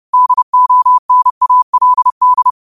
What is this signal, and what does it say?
Morse code, heard as a keyed tone, about 30 wpm, 1000 Hz. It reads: DONALD
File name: qotdfiles / 30wpm